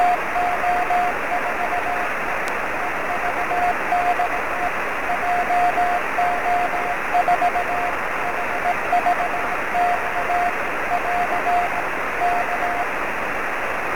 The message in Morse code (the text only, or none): none